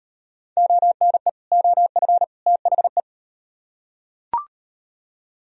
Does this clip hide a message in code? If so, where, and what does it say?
Yes: Morse code from 0.6 to 3.0 s, heard as a keyed tone, about 38 wpm, 700 Hz. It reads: ONE OF THE